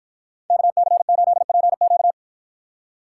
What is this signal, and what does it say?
Morse code, heard as a keyed tone, about 50 wpm, 700 Hz. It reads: KC8PQ